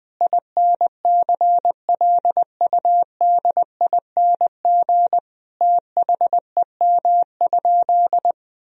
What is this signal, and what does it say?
Morse code, heard as a keyed tone, about 20 wpm, 700 Hz. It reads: INCLUDING THEM?
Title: Sentences from Top 1000 Words - Review Part 4 - 20wpm